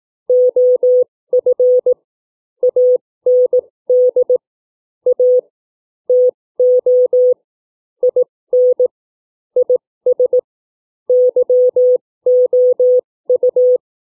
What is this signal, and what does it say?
Morse code, heard as a keyed tone, about 18 wpm, spaced as for 15 wpm, 500 Hz. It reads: OF AND A TO IN IS YOU
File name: CW / book reading 3